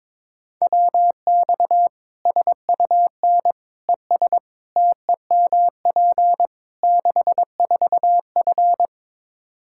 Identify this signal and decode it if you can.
Morse code, heard as a keyed tone, about 22 wpm, 700 Hz. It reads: WX SUN ES TEMP 64F
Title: Practice QSO - Exchange 2 - Info - 22wpm